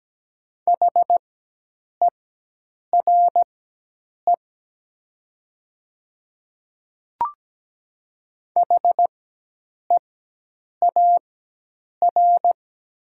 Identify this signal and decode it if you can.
Morse code, heard as a keyed tone, about 17 wpm, spaced as for 8 wpm, 700 Hz. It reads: HERE HEAR
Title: 4x Character Spacing - Top 300 Words - 17wpm